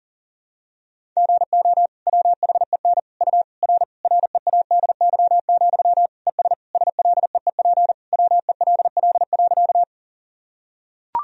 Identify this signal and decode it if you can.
Morse code, heard as a keyed tone, about 40 wpm, 700 Hz. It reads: GO WHEN U R READY, ES SLEEP WELL.
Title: Sentences from Top 500 Words and Abbreviations 40wpm